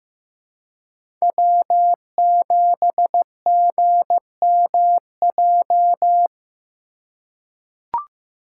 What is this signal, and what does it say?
Morse code, heard as a keyed tone, about 15 wpm, 700 Hz. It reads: W7GMJ
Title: USA Call Signs - 5 Characters - 3x Character Spacing - 15wpm